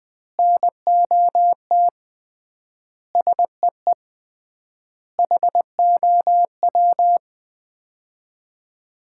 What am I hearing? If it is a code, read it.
Morse code, heard as a keyed tone, about 21 wpm, 700 Hz. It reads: NOT SEE HOW
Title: 3x Word-Spacing - Sets of 5 Words - 20wpm